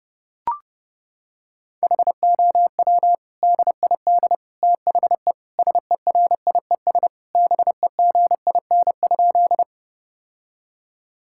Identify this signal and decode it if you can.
Morse code, heard as a keyed tone, about 30 wpm, 700 Hz. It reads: HOW DID THE SERIES BEGIN?
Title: Sentences from Top 1000 Words - Review Part 5 - 30wpm